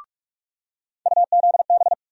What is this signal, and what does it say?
Morse code, heard as a keyed tone, about 46 wpm, 700 Hz. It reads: UZB
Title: USA Call Sign Postfixes - 3 Letters 45wpm